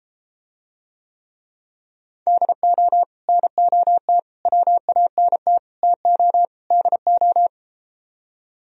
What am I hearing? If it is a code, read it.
Morse code, heard as a keyed tone, about 32 wpm, 700 Hz. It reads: DO NOT WANT TO DO